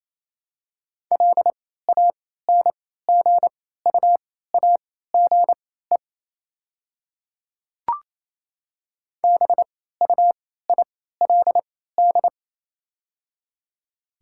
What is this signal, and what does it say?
Morse code, heard as a keyed tone, about 28 wpm, spaced as for 16 wpm, 700 Hz. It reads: LANGUAGE BUILD
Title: 3x Character Spacing - Top 500 Words - 28wpm